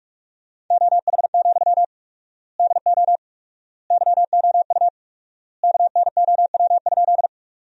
Morse code, heard as a keyed tone, about 46 wpm, 700 Hz. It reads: OH, DO YOU KNOW?
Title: 4x Word-Spacing - Sentences from Top 500 Words - 45wpm